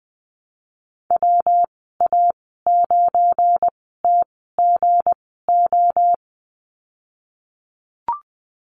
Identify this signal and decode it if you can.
Morse code, heard as a keyed tone, about 20 wpm, spaced as for 15 wpm, 700 Hz. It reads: WA9TGO
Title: 193 Lesson 31g - Letter v - Call Signs 20wpm